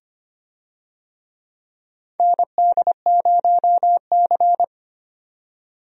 Morse code, heard as a keyed tone, about 25 wpm, 700 Hz. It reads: ND0C
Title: USA Call Signs - 4 Characters - 25wpm